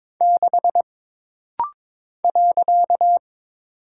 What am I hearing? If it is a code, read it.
Morse code, heard as a keyed tone, about 22 wpm, 700 Hz. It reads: 6 .
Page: This rapid-fire timing is geared to push your ICR (Instant Character Recognition) skills.